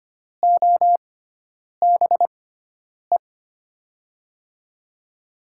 Morse code, heard as a keyed tone, about 25 wpm, spaced as for 9 wpm, 700 Hz. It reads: OBE
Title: USA Call Signs - 6x Character Spacing - 25wpm